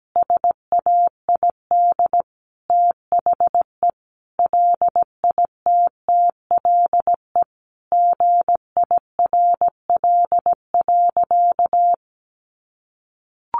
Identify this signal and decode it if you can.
Morse code, heard as a keyed tone, about 17 wpm, 700 Hz. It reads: SAID THE LITTLE GIRL.
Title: Sentences from Top 1200 Words 17wpm